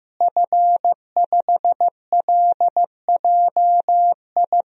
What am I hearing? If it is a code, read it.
Morse code, heard as a keyed tone, about 15 wpm, 700 Hz. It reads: F5LJI
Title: DXCC-100 Call Signs - 5 Characters - Repeated 3x - 15wpm